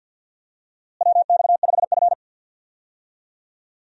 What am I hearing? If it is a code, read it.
Morse code, heard as a keyed tone, about 51 wpm, 700 Hz. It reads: WX5F
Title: USA Call Signs - 4 Characters - Repeated 3x - 50wpm